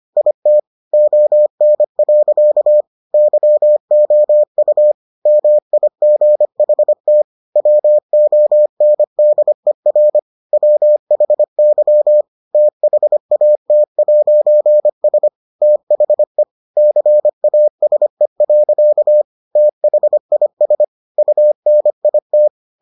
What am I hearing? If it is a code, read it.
Morse code, heard as a keyed tone, about 25 wpm, 600 Hz. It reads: IT ON. YOU MIGHT WONDER WHY THAT'S THE CASE. THIS UNIT